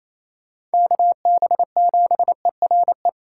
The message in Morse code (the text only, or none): KB7ERE